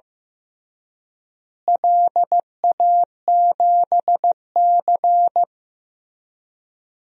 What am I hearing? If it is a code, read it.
Morse code, heard as a keyed tone, about 15 wpm, 700 Hz. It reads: LA7C
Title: DXCC-100 Call Signs - 4 Characters - Repeated 3x - 15wpm